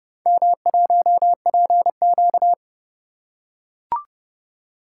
Morse code, heard as a keyed tone, about 29 wpm, 700 Hz. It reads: M1PQ